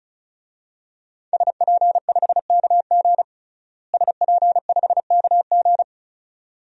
Morse code, heard as a keyed tone, about 34 wpm, 700 Hz. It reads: SP5KG SP5KG
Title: DXCC-100 Call Signs - 5 Characters - Repeated 2x - 35wpm